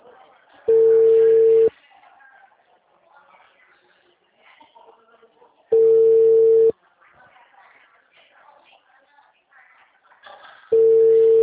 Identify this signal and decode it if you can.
Ringback tone (China, 450 Hz), with speech.